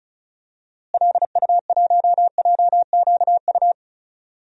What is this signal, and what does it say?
Morse code, heard as a keyed tone, about 36 wpm, 700 Hz. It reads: LU1JQU
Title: DXCC-100 Call Signs - 6 Characters - Repeated 3x - 35wpm